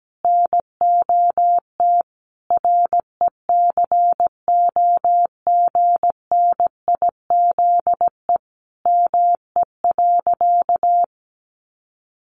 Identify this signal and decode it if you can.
Morse code, heard as a keyed tone, about 17 wpm, 700 Hz. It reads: NOT RECOGNIZE ME.
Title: Sentences from Top 1400 Words 17wpm